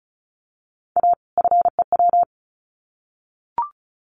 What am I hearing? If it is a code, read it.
Morse code, heard as a keyed tone, about 34 wpm, 700 Hz. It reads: A FEW
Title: Sets of 2 Words 35wpm